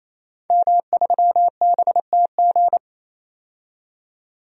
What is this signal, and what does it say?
Morse code, heard as a keyed tone, about 28 wpm, 700 Hz. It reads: M3BTG
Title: DXCC-100 Call Signs - 5 Characters - 28wpm